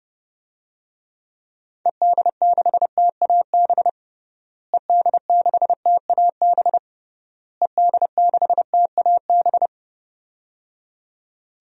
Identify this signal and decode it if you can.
Morse code, heard as a keyed tone, about 28 wpm, 700 Hz. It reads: ED6TAB ED6TAB ED6TAB
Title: DXCC-100 Call Signs - 6 Characters - Repeated 3x - 30wpm